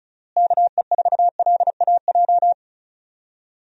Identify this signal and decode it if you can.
Morse code, heard as a keyed tone, about 36 wpm, 700 Hz. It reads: KE4LAJ
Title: USA Call Signs - 6 Characters - Repeated 3x - 35wpm